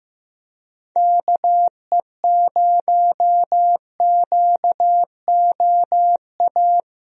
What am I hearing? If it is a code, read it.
Morse code, heard as a keyed tone, about 15 wpm, 700 Hz. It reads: KE0QOA
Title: USA Call Signs - 6 Characters - Repeated 3x - 15wpm